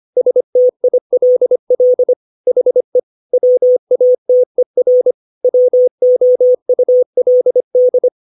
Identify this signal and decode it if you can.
Morse code, heard as a keyed tone, about 25 wpm, 500 Hz. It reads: STILL HE WATER WOULD
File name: 632 words / 632 Words 25WPM